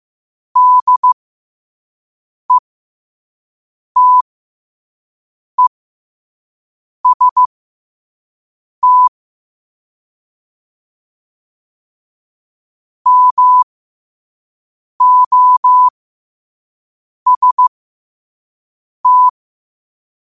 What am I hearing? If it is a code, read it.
Morse code, heard as a keyed tone, about 15 wpm, spaced as for 5 wpm, 1000 Hz. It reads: DETEST MOST
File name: qotdfiles / 5wpm